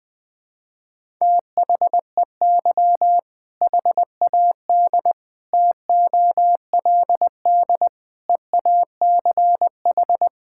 Morse code, heard as a keyed tone, about 20 wpm, 700 Hz. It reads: THEY HAD TOLD EACH
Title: Sentences from Top 1600 Words 20wpm